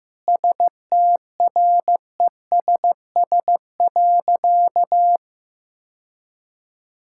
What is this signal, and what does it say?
Morse code, heard as a keyed tone, about 15 wpm, 700 Hz. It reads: STRESS.
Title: Sentences from Top 1000 Words - Review Part 6 - 15wpm